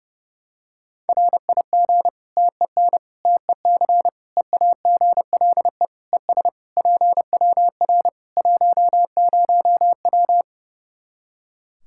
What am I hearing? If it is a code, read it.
Morse code, heard as a keyed tone, about 32 wpm, 700 Hz. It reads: RIG TEN TEC EAGLE ES PWR 10W
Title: K1USN SST QSO Practice 30wpm